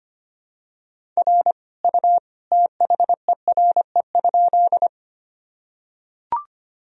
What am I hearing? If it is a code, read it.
Morse code, heard as a keyed tone, about 25 wpm, 700 Hz. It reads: R U THERE?
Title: Sentences from Top 100 Words and Abbreviations 25wpm